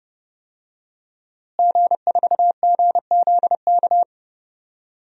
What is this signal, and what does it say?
Morse code, heard as a keyed tone, about 30 wpm, 700 Hz. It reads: G4GZK
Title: DXCC-100 Call Signs - 5 Characters - Repeated 2x - 30wpm